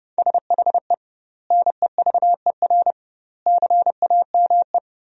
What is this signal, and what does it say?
Morse code, heard as a keyed tone, about 30 wpm, 700 Hz. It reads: SHE NEVER CAME
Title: 2x Word-Spacing - Sentences from Top 300 Words - 30wpm